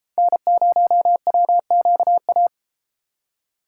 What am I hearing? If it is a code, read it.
Morse code, heard as a keyed tone, about 30 wpm, 700 Hz. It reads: N0WQA